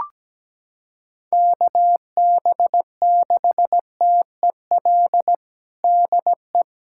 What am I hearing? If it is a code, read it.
Morse code, heard as a keyed tone, about 17 wpm, 700 Hz. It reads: KB6TEL DE
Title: USA Callsign DE Callsign 17wpm